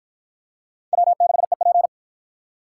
Morse code, heard as a keyed tone, about 53 wpm, 700 Hz. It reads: W6EP